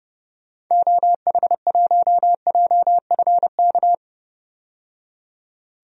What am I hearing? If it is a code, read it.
Morse code, heard as a keyed tone, about 30 wpm, 700 Hz. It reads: OH1JFK
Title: DXCC-100 Call Signs - 6 Characters - 30wpm